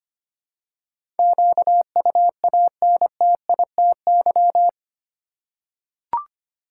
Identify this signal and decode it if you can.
Morse code, heard as a keyed tone, about 25 wpm, 700 Hz. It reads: QUANTITY
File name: 8 Letter Words - First 3 Letters Given - 25wpm